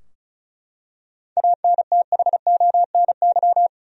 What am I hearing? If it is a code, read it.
Morse code, heard as a keyed tone, about 36 wpm, 700 Hz. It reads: ANTHONY
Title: USA Common Names - 35wpm